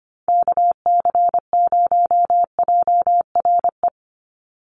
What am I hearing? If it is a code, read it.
Morse code, heard as a keyed tone, about 25 wpm, 700 Hz. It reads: KC0JRE
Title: USA Call Signs 25wpm